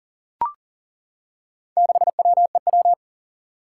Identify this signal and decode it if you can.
Morse code, heard as a keyed tone, about 39 wpm, 700 Hz. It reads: 6WEW
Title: Four Challenging Letter-Number Combinations 40wpm